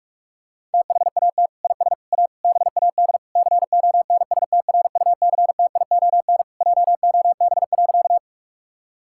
Morse code, heard as a keyed tone, about 46 wpm, 700 Hz. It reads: THAT IS A BAD CONSTRUCTION JOB.
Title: Sentences from Top 1600 Words 45wpm